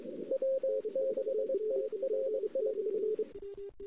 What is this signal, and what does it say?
Morse code, heard as a keyed tone, about 22 wpm, 530 Hz. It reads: W6IRI
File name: CW / 2008-ss-cw-k5tr-op-n5rz -01:00